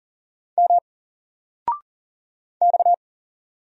Morse code, heard as a keyed tone, about 44 wpm, 700 Hz. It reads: M X
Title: Single Character (Letter:Number:Punctuation:Prosign) 40wpm